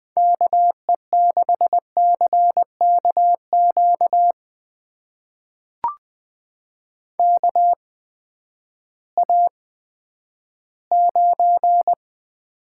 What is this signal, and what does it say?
Morse code, heard as a keyed tone, about 20 wpm, spaced as for 11 wpm, 700 Hz. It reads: KE6CKQ KA9
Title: USA Call Signs - 6 Characters - 8x Character Spacing - 20wpm